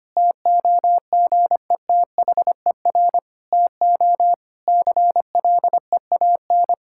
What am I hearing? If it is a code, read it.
Morse code, heard as a keyed tone, about 25 wpm, 700 Hz. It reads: TOGETHER TO CLEAN
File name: practice / 25wpm / Difficult Sentences with Hints - 25wpm